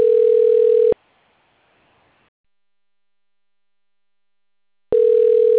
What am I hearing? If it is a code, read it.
Ringback tone (North America), 440 + 480 Hz.